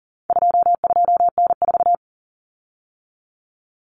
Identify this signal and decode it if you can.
Morse code, heard as a keyed tone, about 40 wpm, 700 Hz. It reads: 22N4
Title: Common Cut Numbers - 4-Digit 40wpm